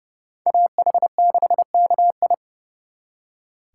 Morse code, heard as a keyed tone, about 38 wpm, 700 Hz. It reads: AH6KI